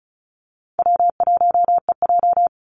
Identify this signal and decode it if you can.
Morse code, heard as a keyed tone, about 34 wpm, 700 Hz. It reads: W1EJ
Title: NA SOTA Activator Callsigns 35wpm